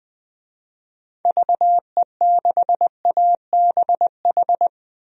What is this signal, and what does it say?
Morse code, heard as a keyed tone, about 20 wpm, 700 Hz. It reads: VE6ABH
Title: DXCC-100 Call Signs - 6 Characters - 20wpm